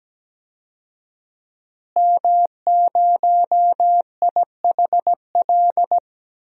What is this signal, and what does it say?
Morse code, heard as a keyed tone, about 17 wpm, 700 Hz. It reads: M0IHL
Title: DXCC-100 Call Signs - 5 Characters - Repeated 3x - 17wpm